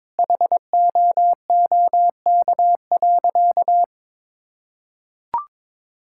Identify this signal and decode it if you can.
Morse code, heard as a keyed tone, about 21 wpm, 700 Hz. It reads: HOOK.
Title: Sentences from Top 1600 Words 22wpm